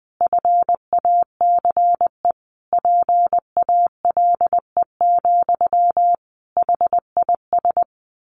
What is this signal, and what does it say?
Morse code, heard as a keyed tone, about 20 wpm, 700 Hz. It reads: FACE PALE, HIS
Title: A Sound Of Thunder 20wpm